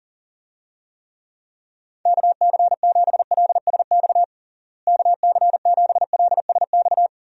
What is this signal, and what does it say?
Morse code, heard as a keyed tone, about 40 wpm, 700 Hz. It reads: KC7LSX KC7LSX
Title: USA Call Signs - 6 Characters - Repeated 3x - 40wpm